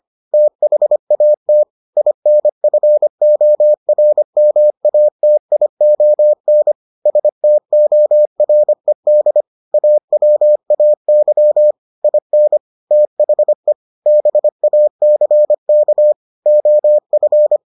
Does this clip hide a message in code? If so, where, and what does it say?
Yes: Morse code from 0.3 to 17.6 s, heard as a keyed tone, about 25 wpm, 600 Hz. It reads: THAT INFORMATION STORED AWAY IN THE BACK OF